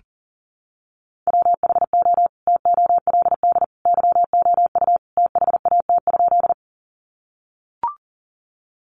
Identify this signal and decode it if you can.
Morse code, heard as a keyed tone, about 40 wpm, 700 Hz. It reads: WHO TOLD YOU THAT?
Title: Sentences from Top 400 Words 40wpm